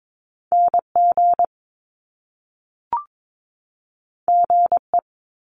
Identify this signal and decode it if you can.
Morse code, heard as a keyed tone, about 22 wpm, 700 Hz. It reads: NG GE
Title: Top Two Letter Combinations in Words 22wpm